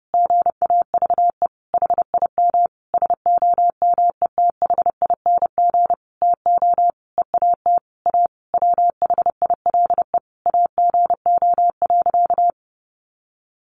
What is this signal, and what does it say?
Morse code, heard as a keyed tone, about 30 wpm, 700 Hz. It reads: GAVE HIM SOMETHING TO EAT A WHILE AGO.
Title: Sentences from Top 600 Words 30wpm